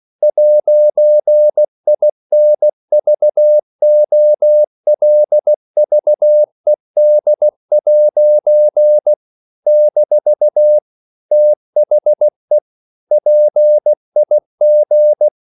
Morse code, heard as a keyed tone, about 16 wpm, 600 Hz. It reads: 'INVOLVED' - THE PIG
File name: CW / Practice / EnglishQuotations0116